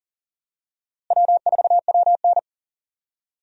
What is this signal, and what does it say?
Morse code, heard as a keyed tone, about 40 wpm, 700 Hz. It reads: W4WN
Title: USA Call Signs - 4 Characters - Repeated 2x - 40wpm